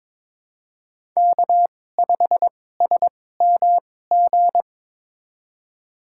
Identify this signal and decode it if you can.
Morse code, heard as a keyed tone, about 22 wpm, spaced as for 16 wpm, 700 Hz. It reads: K5SMG
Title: USA Call Signs - 5 Characters - 2x Character Spacing - 22wpm